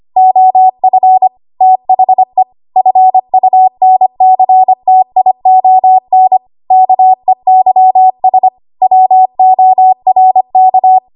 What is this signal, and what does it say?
Morse code, heard as a keyed tone, about 25 wpm, 750 Hz. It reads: OF THE FUNCTION KEYS WORK
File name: Archive / 25 WPM / 25WPM